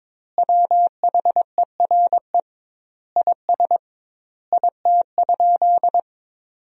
Morse code, heard as a keyed tone, about 22 wpm, 700 Hz. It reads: WHERE IS IT?
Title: 2x Word-Spacing - Sentences from Top 200 Words - 22wpm